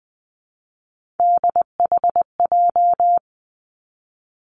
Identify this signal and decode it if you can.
Morse code, heard as a keyed tone, about 20 wpm, 700 Hz. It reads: DHJ
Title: Three Challenging Letter-Number Combinations 20wpm